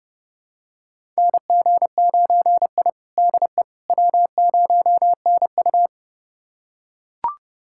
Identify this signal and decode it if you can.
Morse code, heard as a keyed tone, about 31 wpm, 700 Hz. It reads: NG9I DE W0NU
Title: USA Callsign DE Callsign - 4 Characters - 30wpm